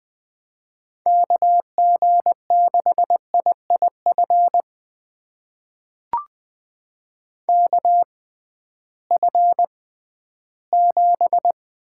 Morse code, heard as a keyed tone, about 21 wpm, spaced as for 13 wpm, 700 Hz. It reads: KG6IIF KF7
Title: USA Call Signs - 6 Characters - 6x Character Spacing - 20wpm